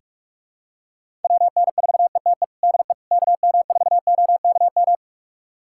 Morse code, heard as a keyed tone, about 41 wpm, 700 Hz. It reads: WN4ETE DE KM4OKM
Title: USA Callsign DE Callsign 45wpm